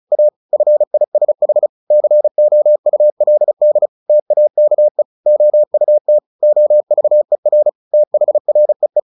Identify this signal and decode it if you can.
Morse code, heard as a keyed tone, about 34 wpm, 600 Hz. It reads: A FISH COULD TAKE OUT OVER THREE
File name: the old man and the sea / the old man and the sea 1 35wpm